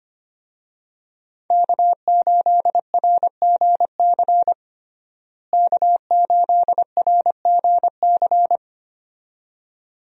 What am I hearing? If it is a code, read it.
Morse code, heard as a keyed tone, about 25 wpm, 700 Hz. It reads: K8RGC K8RGC
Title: USA Call Signs - 5 Characters - Repeated 2x - 25wpm